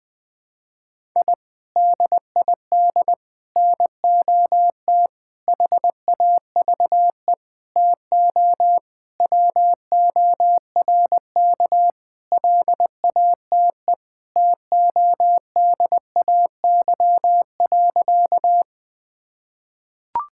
Morse code, heard as a keyed tone, about 20 wpm, 700 Hz. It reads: I DID NOT HAVE TO WORK LATE TODAY.
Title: Sentences with 5-Word Phrases - Spoken Beginning - 20wpm